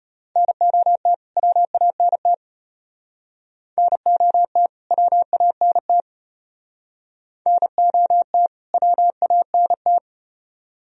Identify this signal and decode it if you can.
Morse code, heard as a keyed tone, about 35 wpm, 700 Hz. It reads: NOT WANT NOT WANT NOT WANT